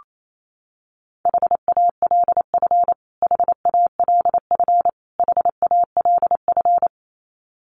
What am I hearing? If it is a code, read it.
Morse code, heard as a keyed tone, about 28 wpm, 700 Hz. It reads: HALF HALF HALF
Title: Top 500 Words Repeated 28wpm